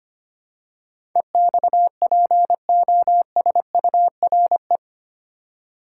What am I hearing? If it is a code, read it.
Morse code, heard as a keyed tone, about 25 wpm, 690 Hz. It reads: EXPOSURE
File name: 8 Letter Words - First 3 Letters Given - 25wpm